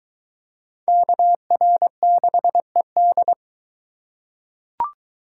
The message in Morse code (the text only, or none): KR6ED